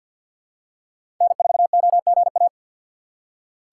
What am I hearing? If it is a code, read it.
Morse code, heard as a keyed tone, about 47 wpm, 700 Hz. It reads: N4OGA
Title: USA Call Signs - 5 Characters - 3x Character Spacing - 50wpm